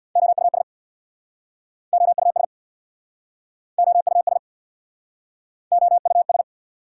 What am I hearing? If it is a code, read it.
Morse code, heard as a keyed tone, about 69 wpm, 700 Hz. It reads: OUS OUS OUS OUS